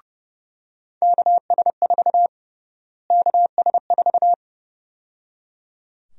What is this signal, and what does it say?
Morse code, heard as a keyed tone, about 31 wpm, 700 Hz. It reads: KS4 KS4
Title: USA Call Sign Prefixes - Repeated 2x - 30wpm